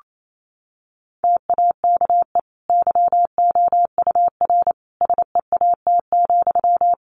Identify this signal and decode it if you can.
Morse code, heard as a keyed tone, about 28 wpm, 700 Hz. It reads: TAKE YOUR SEAT,
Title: Sentences from Top 1500 Words 28wpm